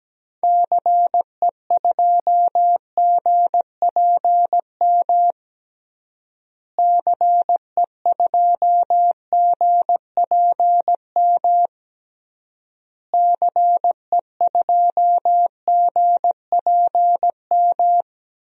Morse code, heard as a keyed tone, about 17 wpm, 700 Hz. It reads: CE2GPM CE2GPM CE2GPM
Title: DXCC-100 Call Signs - 6 Characters - Repeated 3x - 17wpm